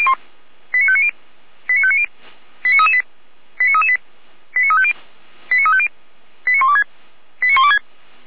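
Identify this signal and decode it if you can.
Five-tone selective call (ZVEI), heard as quick runs of five tones, tones of about 70 ms. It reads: 89590 89590 89338 89338 89350 89350 89126 89126
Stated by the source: Five-tone ZVEI selective call received by SDR.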